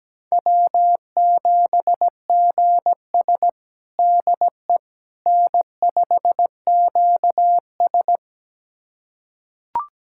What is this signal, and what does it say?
Morse code, heard as a keyed tone, about 17 wpm, 700 Hz. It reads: W7GS DE N5QS
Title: USA Callsign DE Callsign - 4 Characters - 17wpm